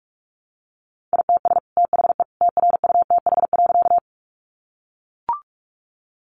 Morse code, heard as a keyed tone, about 44 wpm, 700 Hz. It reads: ITS THE TRUTH.
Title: Sentences from Top 800 Words 45wpm